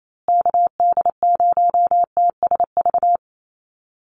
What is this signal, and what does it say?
Morse code, heard as a keyed tone, about 28 wpm, 700 Hz. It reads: KD0TSV
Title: USA Call Signs Repeated 2x 28wpm